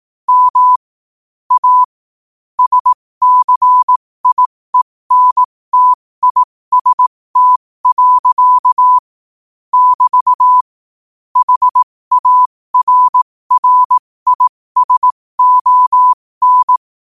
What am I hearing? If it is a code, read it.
Morse code, heard as a keyed tone, about 18 wpm, spaced as for 15 wpm, 1000 Hz. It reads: M A SCIENTIST. <BT> HARRISON
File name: qotdfiles / 15wpm